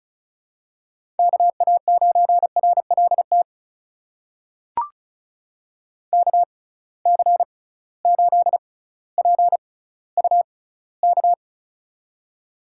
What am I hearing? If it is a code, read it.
Morse code, heard as a keyed tone, about 34 wpm, spaced as for 18 wpm, 700 Hz. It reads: KA9RLT KC8PUK
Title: USA Call Signs - 6 Characters - 6x Character Spacing - 35wpm